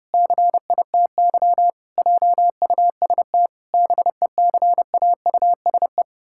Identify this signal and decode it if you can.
Morse code, heard as a keyed tone, about 30 wpm, 700 Hz. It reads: CITY JUST BECAUSE
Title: Pangram Challenge v2 - 30wpm